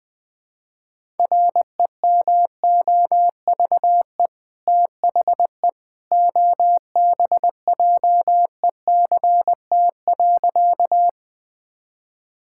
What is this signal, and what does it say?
Morse code, heard as a keyed tone, about 20 wpm, 700 Hz. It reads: REMOVE THE OBJECT.
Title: Sentences from Top 1000 Words - Review Part 5 - 20wpm